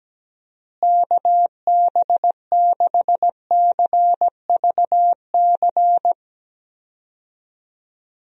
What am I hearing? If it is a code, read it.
Morse code, heard as a keyed tone, about 17 wpm, 700 Hz. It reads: KB6CVC
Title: USA Call Signs - 6 Characters - Repeated 2x - 17wpm